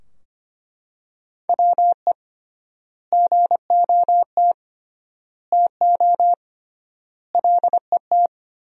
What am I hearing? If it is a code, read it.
Morse code, heard as a keyed tone, about 25 wpm, 700 Hz. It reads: WE GOT TO LET
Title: 3x Word-Spacing - Sentences from Top 300 Words - 25wpm